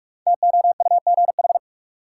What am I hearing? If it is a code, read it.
Morse code, heard as a keyed tone, about 46 wpm, 700 Hz. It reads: TOUGH
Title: Sentences from Top 1200 Words - New Word Drill Alt - 45wpm